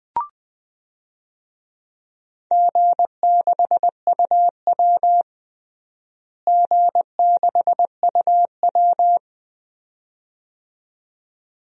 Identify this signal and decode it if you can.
Morse code, heard as a keyed tone, about 20 wpm, 700 Hz. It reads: G6UW G6UW
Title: DXCC-100 Call Signs - 4 Characters - Repeated 2x - 20wpm